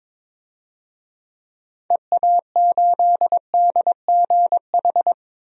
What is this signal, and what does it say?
Morse code, heard as a keyed tone, about 22 wpm, 700 Hz. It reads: EA8DGH
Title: DXCC-100 Call Signs - 6 Characters - 22wpm